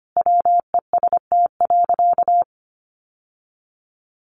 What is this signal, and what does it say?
Morse code, heard as a keyed tone, about 25 wpm, 700 Hz. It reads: WEST.
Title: Sentences from Top 1200 Words 25wpm